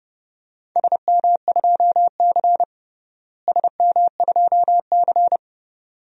Morse code, heard as a keyed tone, about 30 wpm, 700 Hz. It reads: SM2C SM2C
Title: DXCC-100 Call Signs - 4 Characters - Repeated 2x - 30wpm